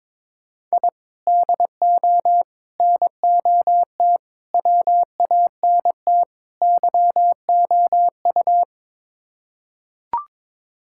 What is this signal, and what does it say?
Morse code, heard as a keyed tone, about 22 wpm, 700 Hz. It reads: I DO NOT WANT YOU
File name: practice / 22wpm / Build A Phrase - 5-Word Phrases - 22wpm